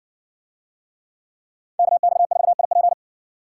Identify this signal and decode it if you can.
Morse code, heard as a keyed tone, about 60 wpm, 700 Hz. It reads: KX4IP